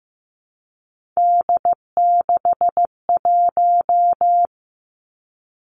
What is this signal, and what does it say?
Morse code, heard as a keyed tone, about 15 wpm, 700 Hz. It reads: D61
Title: Three Challenging Letter-Number Combinations 15wpm